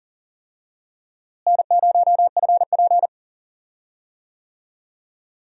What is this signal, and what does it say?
Morse code, heard as a keyed tone, about 40 wpm, 700 Hz. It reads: N0FP
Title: USA Call Signs - 4 Characters - Repeated 3x - 40wpm